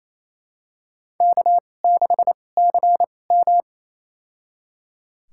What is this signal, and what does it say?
Morse code, heard as a keyed tone, about 28 wpm, spaced as for 20 wpm, 700 Hz. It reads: K6CM
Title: USA Call Signs - 4 Characters - 2x Character Spacing - 28wpm